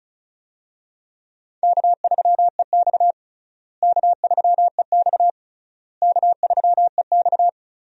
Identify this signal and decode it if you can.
Morse code, heard as a keyed tone, about 36 wpm, 700 Hz. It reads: K3EX K3EX K3EX
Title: USA Call Signs - 4 Characters - Repeated 3x - 35wpm